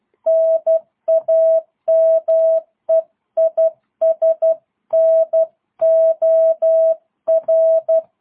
Morse code, heard as a keyed tone, about 12 wpm, 650 Hz. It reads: NAME IS NOR
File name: jh8cbh / 2022cw